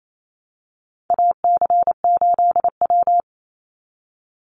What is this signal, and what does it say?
Morse code, heard as a keyed tone, about 28 wpm, 700 Hz. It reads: AC8W
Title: Call Signs Repeated 2x - 28wpm